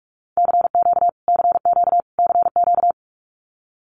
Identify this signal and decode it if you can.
Morse code, heard as a keyed tone, about 44 wpm, 700 Hz. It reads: CQ CQ CQ
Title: Abbreviations Repeated 3x - 45wpm